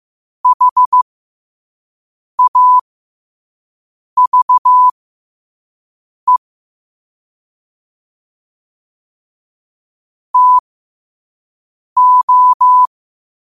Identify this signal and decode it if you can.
Morse code, heard as a keyed tone, about 15 wpm, spaced as for 5 wpm, 1000 Hz. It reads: HAVE TO
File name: qotdfiles / 5wpm